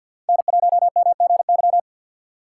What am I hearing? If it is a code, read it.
Morse code, heard as a keyed tone, about 53 wpm, 700 Hz. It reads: N1MGY
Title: USA Call Signs - 5 Characters - 4x Character Spacing - 50wpm